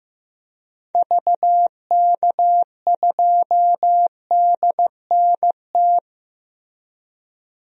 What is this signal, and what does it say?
Morse code, heard as a keyed tone, about 15 wpm, 700 Hz. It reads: VK2DNT
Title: DXCC-100 Call Signs - 6 Characters - Repeated 3x - 15wpm